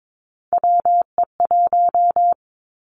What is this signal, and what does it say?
Morse code, heard as a keyed tone, about 22 wpm, 700 Hz. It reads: WE1
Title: Three Challenging Letter-Number Combinations 22wpm